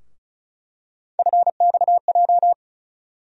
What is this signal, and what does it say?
Morse code, heard as a keyed tone, about 36 wpm, 700 Hz. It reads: FXJ
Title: USA Call Sign Postfixes - 3 Letters 35wpm